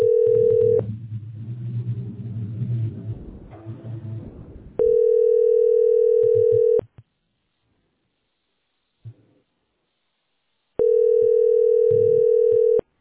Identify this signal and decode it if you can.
Ringback tone (North America), 440 + 480 Hz.